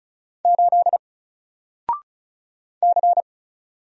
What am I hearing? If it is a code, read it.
Morse code, heard as a keyed tone, about 36 wpm, 700 Hz. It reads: 8 C
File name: Single-Letter-Number-35wpm